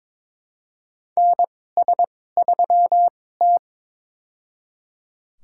Morse code, heard as a keyed tone, about 22 wpm, spaced as for 16 wpm, 700 Hz. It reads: NS3T
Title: USA Call Signs - 4 Characters - 2x Character Spacing - 22wpm